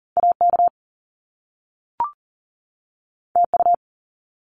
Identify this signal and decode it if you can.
Morse code, heard as a keyed tone, about 39 wpm, 700 Hz. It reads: AK TU
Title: Top Two Letter Combinations in Words 40wpm